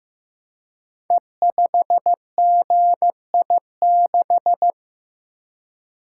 Morse code, heard as a keyed tone, about 15 wpm, 700 Hz. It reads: E5GI6
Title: Code Groups - 5 Letter:Numbers - 15wpm